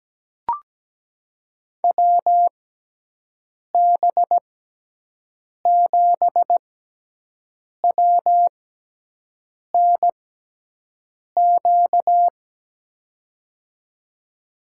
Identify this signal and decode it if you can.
Morse code, heard as a keyed tone, about 17 wpm, spaced as for 6 wpm, 700 Hz. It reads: WB7WNQ
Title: USA Call Signs - 6 Characters - 6x Character Spacing - 17wpm